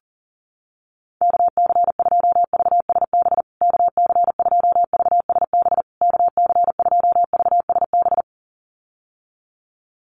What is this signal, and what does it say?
Morse code, heard as a keyed tone, about 40 wpm, 700 Hz. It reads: KC2VSB KC2VSB KC2VSB
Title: USA Call Signs Repeated 3x 40wpm